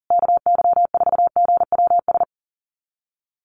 Morse code, heard as a keyed tone, about 37 wpm, 700 Hz. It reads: KY4GWS
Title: USA Call Signs Repeated 3x 40wpm